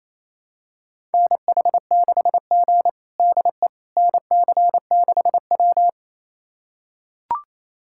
Morse code, heard as a keyed tone, about 28 wpm, 700 Hz. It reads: NH6G DE NC6W
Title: USA Callsign DE Callsign - 4 Characters - 28wpm